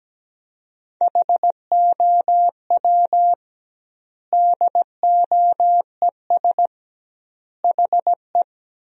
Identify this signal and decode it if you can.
Morse code, heard as a keyed tone, about 17 wpm, 700 Hz. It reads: HOW DOES HE
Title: 2x Word-Spacing - Sentences from Top 200 Words - 17wpm